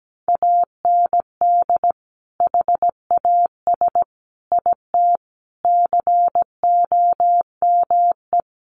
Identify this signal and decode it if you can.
Morse code, heard as a keyed tone, about 17 wpm, 700 Hz. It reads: AND HAS IT COME
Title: Sentences from Top 100 Words 17wpm